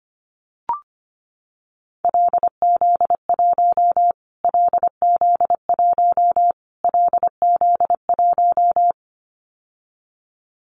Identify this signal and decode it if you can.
Morse code, heard as a keyed tone, about 25 wpm, 700 Hz. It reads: LZ1 LZ1 LZ1
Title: Callsign Prefixes for Top 25 DXCC - Repeated 3x - 25wpm